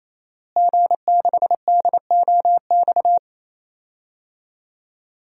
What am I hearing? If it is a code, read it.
Morse code, heard as a keyed tone, about 27 wpm, 700 Hz. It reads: G6DOX
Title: DXCC-100 Call Signs - 5 Characters - Repeated 3x - 28wpm